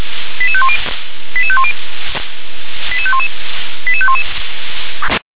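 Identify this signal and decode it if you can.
Five-tone selective call (ZVEI), heard as quick runs of five tones, tones of about 70 ms. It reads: 88510 88510 88511 88511
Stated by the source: ZVEI five-tone selective call received on SDR.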